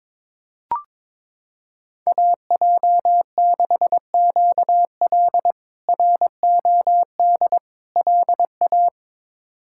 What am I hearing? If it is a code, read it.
Morse code, heard as a keyed tone, about 21 wpm, 700 Hz. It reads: AJ6QL ROD LA
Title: K1USN SST QSO Practice - 22wpm